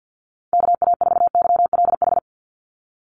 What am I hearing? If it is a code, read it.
Morse code, heard as a keyed tone, about 48 wpm, 700 Hz. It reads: KA4YLH
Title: USA Call Signs 50wpm